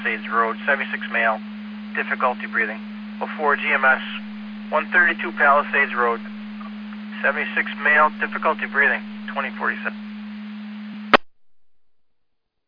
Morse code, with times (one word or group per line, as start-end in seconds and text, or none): none